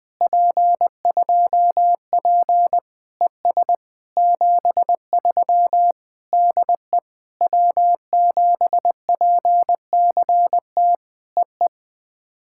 Morse code, heard as a keyed tone, about 20 wpm, 700 Hz. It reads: P2P ES 73 DE W7PCT EE
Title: POTA QSO Practice - 20wpm